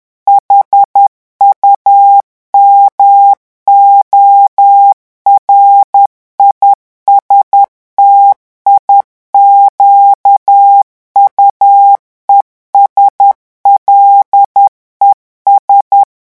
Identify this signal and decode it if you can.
Morse code, heard as a keyed tone, about 10 wpm, 800 Hz. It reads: HUMORISTIQUES LES
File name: morse mp3 / revision9 citations humoristiques vitesse 10 mots minute